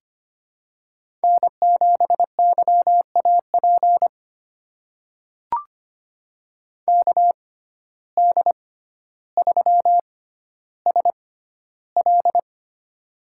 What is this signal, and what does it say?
Morse code, heard as a keyed tone, about 25 wpm, spaced as for 13 wpm, 700 Hz. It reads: N7YAP KD3SL
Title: USA Call Signs - 5 Characters - 6x Character Spacing - 25wpm